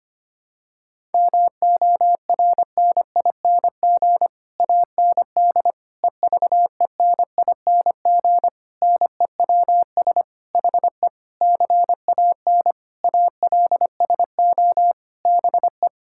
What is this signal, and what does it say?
Morse code, heard as a keyed tone, about 25 wpm, 700 Hz. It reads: MORNING AND EVENING NEWS HE CAN ALSO BE
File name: Sets of 9 Words - 25wpm